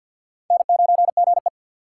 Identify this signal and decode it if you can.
Morse code, heard as a keyed tone, about 48 wpm, 700 Hz. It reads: N9GE
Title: USA Call Signs - 4 Characters - 6x Character Spacing - 50wpm